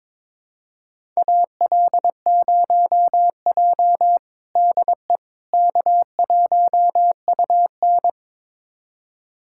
Morse code, heard as a keyed tone, about 22 wpm, 700 Hz. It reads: AL0J DE K1UN
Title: USA Callsign DE Callsign - 4 Characters - 22wpm